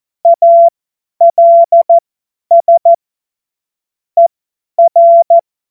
Morse code, heard as a keyed tone, about 14 wpm, spaced as for 10 wpm, 680 Hz. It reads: ALS ER
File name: Morsekurs / Witze / Witz 13 Februar 2024 50 F70